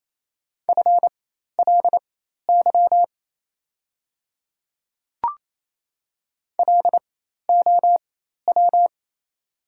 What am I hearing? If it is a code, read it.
Morse code, heard as a keyed tone, about 27 wpm, spaced as for 13 wpm, 700 Hz. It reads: FLY LOW
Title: 4x Character Spacing - Top 500 Words - 28wpm